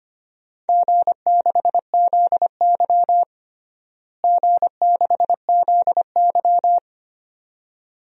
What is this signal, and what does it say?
Morse code, heard as a keyed tone, about 25 wpm, 700 Hz. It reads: G6ZY G6ZY
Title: DXCC-100 Call Signs - 4 Characters - Repeated 2x - 25wpm